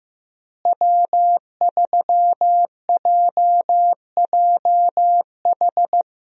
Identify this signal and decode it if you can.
Morse code, heard as a keyed tone, about 15 wpm, 700 Hz. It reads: W3JJH
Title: USA Call Signs - 5 Characters - Repeated 2x - 15wpm